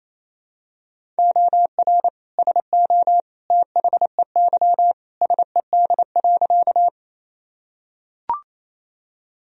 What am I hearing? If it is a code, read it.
Morse code, heard as a keyed tone, about 28 wpm, 700 Hz. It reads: OR SO THEY SED.
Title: Sentences from Top 100 Words and Abbreviations 28wpm